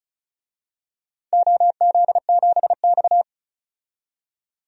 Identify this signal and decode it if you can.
Morse code, heard as a keyed tone, about 36 wpm, 700 Hz. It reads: OZ7X
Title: DXCC-100 Call Signs - 4 Characters - 35wpm